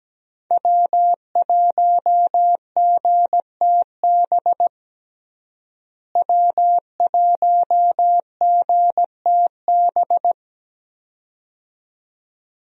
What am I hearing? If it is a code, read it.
Morse code, heard as a keyed tone, about 17 wpm, 700 Hz. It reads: W1GTB W1GTB
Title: USA Call Signs - 5 Characters - Repeated 2x - 17wpm